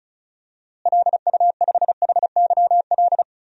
Morse code, heard as a keyed tone, about 34 wpm, 700 Hz. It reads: LU5HYL
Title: DXCC-100 Call Signs - 6 Characters - Repeated 2x - 35wpm